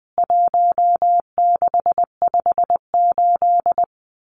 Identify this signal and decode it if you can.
Morse code, heard as a keyed tone, about 20 wpm, 700 Hz. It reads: 1658
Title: CWT Exchange 20wpm